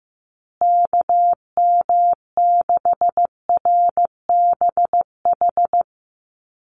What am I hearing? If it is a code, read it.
Morse code, heard as a keyed tone, about 15 wpm, 700 Hz. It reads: KM6RBH
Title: USA Call Signs Repeated 3x 15wpm